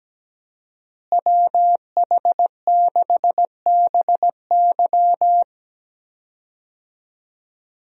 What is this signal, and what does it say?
Morse code, heard as a keyed tone, about 17 wpm, 700 Hz. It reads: WH6BY
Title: USA Call Signs - 5 Characters - Repeated 2x - 17wpm